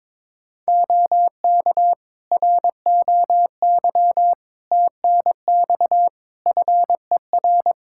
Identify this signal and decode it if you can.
Morse code, heard as a keyed tone, about 22 wpm, 700 Hz. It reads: OK ROY TNX FER
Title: Practice QSO - Exchange 4 - Ending - 22wpm